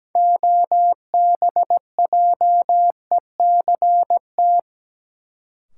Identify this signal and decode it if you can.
Morse code, heard as a keyed tone, about 17 wpm, 700 Hz. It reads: OBJECT
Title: Sentences from Top 500 Words - New Word Drill - 17wpm